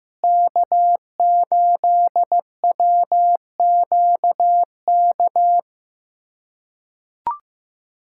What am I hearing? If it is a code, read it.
Morse code, heard as a keyed tone, about 15 wpm, 700 Hz. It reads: K8WQK
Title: USA Callsign DE Callsign 15wpm